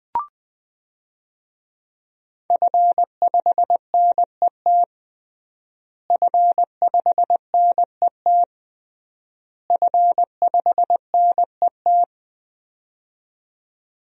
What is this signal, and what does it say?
Morse code, heard as a keyed tone, about 20 wpm, 700 Hz. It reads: F5NET F5NET F5NET
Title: DXCC-100 Call Signs - 5 Characters - Repeated 3x - 20wpm